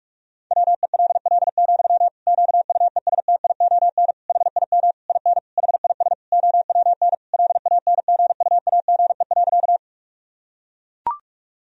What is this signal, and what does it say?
Morse code, heard as a keyed tone, about 41 wpm, 700 Hz. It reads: WELL, QUESTION HIM IN HIS OWN LANGUAGE.
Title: Sentences from Top 1000 Words - Review Part 3 - 45wpm